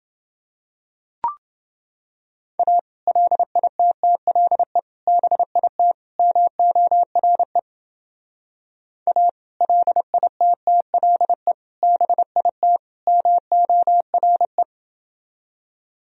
Morse code, heard as a keyed tone, about 28 wpm, 700 Hz. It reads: A LITTLE BIT MORE A LITTLE BIT MORE